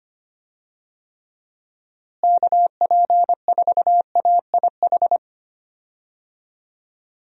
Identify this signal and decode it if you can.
Morse code, heard as a keyed tone, about 25 wpm, 700 Hz. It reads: KP4AIH
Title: DXCC-100 Call Signs - 6 Characters - Repeated 2x - 25wpm